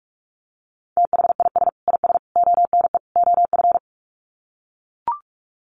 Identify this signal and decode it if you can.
Morse code, heard as a keyed tone, about 44 wpm, 700 Hz. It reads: THIS IS ONE OF
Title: Sets of 4 Words 45wpm